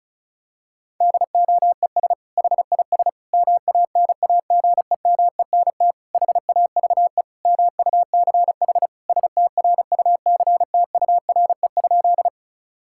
Morse code, heard as a keyed tone, about 34 wpm, 700 Hz. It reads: DOES HIS MANAGEMENT HAVE MUCH STRUCTURE?
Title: Sentences from Top 1000 Words - Review Part 6 - 35wpm